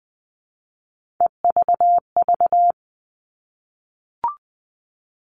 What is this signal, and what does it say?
Morse code, heard as a keyed tone, about 20 wpm, 700 Hz. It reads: EVV
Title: All Cut Numbers - 3-Digit 20wpm